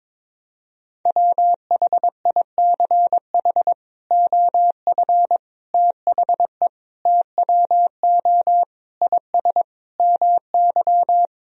Morse code, heard as a keyed tone, about 22 wpm, 700 Hz. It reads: WHICH OF THE TWO IS MY
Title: Sentences from Top 1000 Words - Review Part 2 - 22wpm